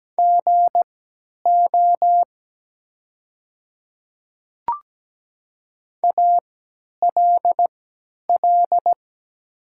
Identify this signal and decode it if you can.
Morse code, heard as a keyed tone, about 17 wpm, spaced as for 10 wpm, 700 Hz. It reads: GO ALL
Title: 3x Character Spacing - Top 100 Words - 17wpm